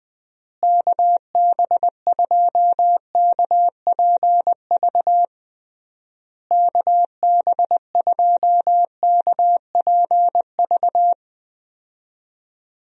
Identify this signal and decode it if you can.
Morse code, heard as a keyed tone, about 20 wpm, 700 Hz. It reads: KB2KPV KB2KPV
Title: USA Call Signs - 6 Characters - Repeated 3x - 20wpm